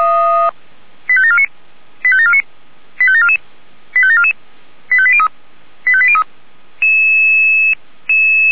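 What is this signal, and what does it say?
Five-tone selective call (ZVEI), heard as quick runs of five tones, tones of about 70 ms. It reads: 85749 85749 85744 85744 85803 85803
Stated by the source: ZVEI five-tone call 85827 received by SDR.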